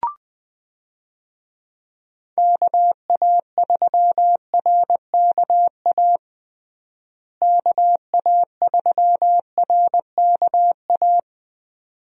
Morse code, heard as a keyed tone, about 20 wpm, 700 Hz. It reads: KA3RKA KA3RKA
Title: USA Call Signs - 6 Characters - Repeated 2x - 20wpm